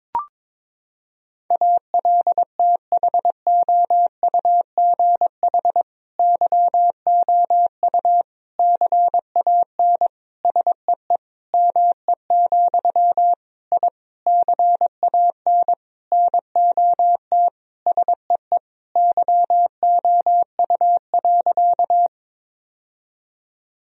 Morse code, heard as a keyed tone, about 22 wpm, 700 Hz. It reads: ALTHOUGH YOU CAN SEE ME, I CAN NOT SEE YOU.
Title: Sentences from Top 1000 Words - Review Part 3 - 22wpm